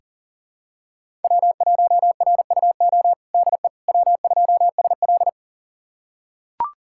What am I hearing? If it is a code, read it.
Morse code, heard as a keyed tone, about 40 wpm, 700 Hz. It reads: W1RUO DE W2SL
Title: USA Callsign DE Callsign 40wpm